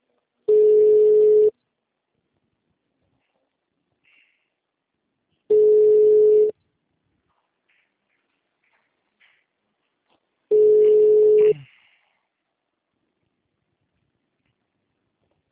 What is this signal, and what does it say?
Ringback tone (Europe, 425 Hz).